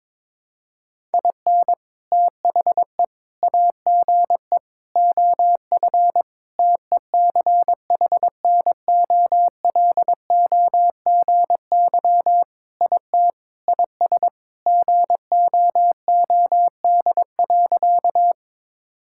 Morse code, heard as a keyed tone, about 21 wpm, 700 Hz. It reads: IN THE AGE OF TECHNOLOGY IT IS GOOD.
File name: practice / 22wpm / Sentences with 3-Word Phrases - 22wpm